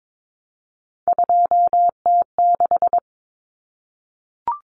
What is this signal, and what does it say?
Morse code, heard as a keyed tone, about 22 wpm, 700 Hz. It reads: 2T6
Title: Common Cut Numbers - 3-Digit 22wpm